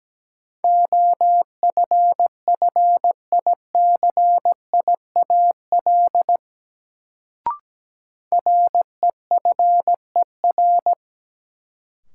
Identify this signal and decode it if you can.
Morse code, heard as a keyed tone, about 17 wpm, 700 Hz. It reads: OFFICIAL REFER
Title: Sentences from Top 1000 Words - New Word Drill - 17wpm